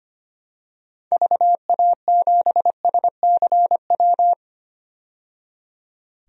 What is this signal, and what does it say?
Morse code, heard as a keyed tone, about 25 wpm, 700 Hz. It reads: VA7SCW
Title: DXCC-100 Call Signs - 6 Characters - 25wpm